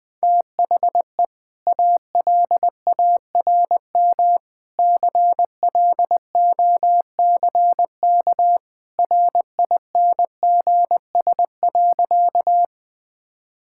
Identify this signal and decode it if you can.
Morse code, heard as a keyed tone, about 19 wpm, 700 Hz. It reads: THE ALARM CLOCK RINGS.
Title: Easy Sentences with Hints - 20wpm